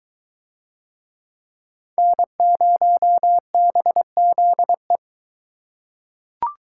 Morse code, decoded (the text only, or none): N0BZE